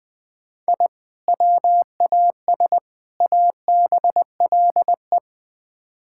Morse code, heard as a keyed tone, about 20 wpm, 700 Hz. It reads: I WAS ABLE
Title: Build A Phrase - 4-Word Phrases - 20wpm